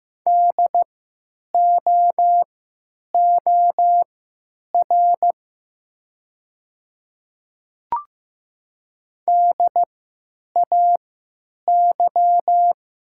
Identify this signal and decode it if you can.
Morse code, heard as a keyed tone, about 15 wpm, spaced as for 9 wpm, 700 Hz. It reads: DOOR DAY
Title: 3x Character Spacing - Top 300 Words - 15wpm